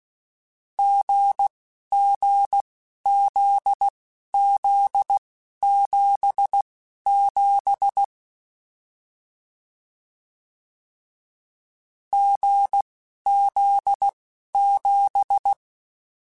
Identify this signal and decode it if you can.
Morse code, heard as a keyed tone, about 16 wpm, spaced as for 11 wpm, 800 Hz. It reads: GGZZ77 GZ7
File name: Difficult Combos